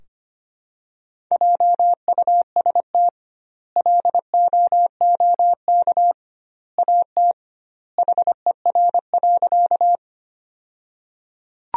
Morse code, heard as a keyed tone, about 25 wpm, 700 Hz. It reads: JUST LOOK AT HER.
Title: 2x Word-Spacing - Sentences from Top 200 Words - 25wpm